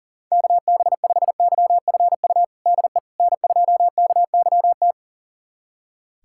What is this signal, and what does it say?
Morse code, heard as a keyed tone, about 40 wpm, 700 Hz. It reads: KB5YFU DE N2KYT
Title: USA Callsign DE Callsign 40wpm